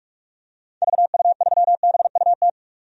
Morse code, heard as a keyed tone, about 41 wpm, 700 Hz. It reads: VU3BUT
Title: DXCC-100 Call Signs - 6 Characters - 45wpm